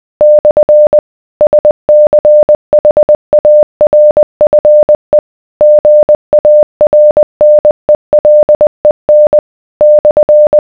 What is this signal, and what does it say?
Morse code, heard as a keyed tone, about 20 wpm, 600 Hz. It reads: / SCHARFE GARNELEN /